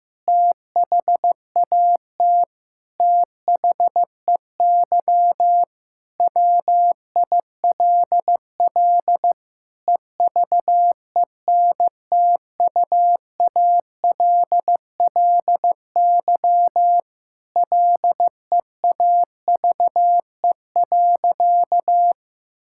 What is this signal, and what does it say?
Morse code, heard as a keyed tone, about 15 wpm, 700 Hz. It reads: THAT THEY WILL EVENTUALLY LEAVE.
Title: Sentences from Top 1000 Words - Review Part 6 - 15wpm